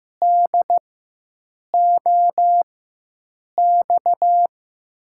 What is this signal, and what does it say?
Morse code, heard as a keyed tone, about 15 wpm, spaced as for 7 wpm, 700 Hz. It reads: DOX
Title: USA Call Signs - 6 Characters - 4x Character Spacing - 15wpm